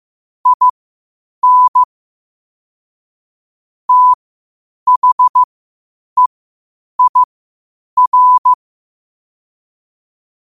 Morse code, heard as a keyed tone, about 15 wpm, spaced as for 8 wpm, 1000 Hz. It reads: IN THEIR
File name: qotdfiles / 8wpm